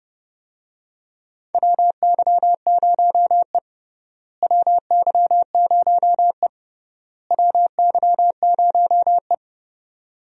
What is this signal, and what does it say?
Morse code, heard as a keyed tone, about 30 wpm, 700 Hz. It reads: WY0E WY0E WY0E
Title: USA Call Signs - 4 Characters - Repeated 3x - 30wpm